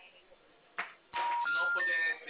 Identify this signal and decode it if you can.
Special information tone, with speech.
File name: out- -2003- - - .71510